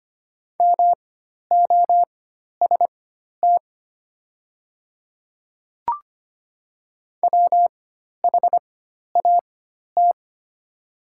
Morse code, heard as a keyed tone, about 25 wpm, spaced as for 12 wpm, 700 Hz. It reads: MOST WHAT
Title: 4x Character Spacing - Top 200 Words - 25wpm